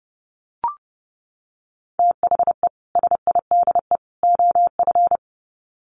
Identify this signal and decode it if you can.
Morse code, heard as a keyed tone, about 30 wpm, 700 Hz. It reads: THE SIDE OF
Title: Sets of 3 Words 30wpm